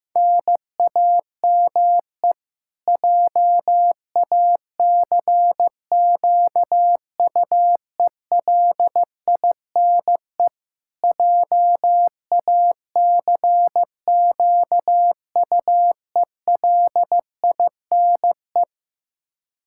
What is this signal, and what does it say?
Morse code, heard as a keyed tone, about 15 wpm, 700 Hz. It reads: NAME JACQUELINE JACQUELINE
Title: K1USN SST QSO Practice 15wpm